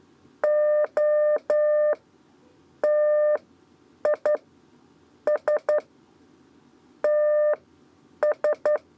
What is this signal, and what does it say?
Morse code, heard as a keyed tone, about 11 wpm, spaced as for 8 wpm, 600 Hz. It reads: O TI S TS
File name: Lezione trasmessa 2 lez 2 exe